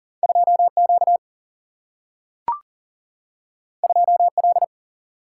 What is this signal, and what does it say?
Morse code, heard as a keyed tone, about 42 wpm, 700 Hz. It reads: 2 Q 2 L
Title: Single-Letter Single-Number 40wpm